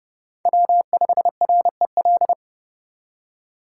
Morse code, heard as a keyed tone, about 34 wpm, 700 Hz. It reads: W5REL
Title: Speed Racing - USA 1x3 Call Signs 30wpm